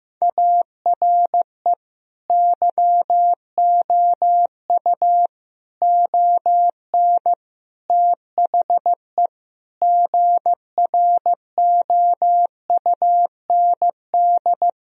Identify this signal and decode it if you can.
Morse code, heard as a keyed tone, about 15 wpm, 700 Hz. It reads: ARE YOU ON THE GROUND
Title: Sentences from Top 1000 Words - Review Part 2 - 15wpm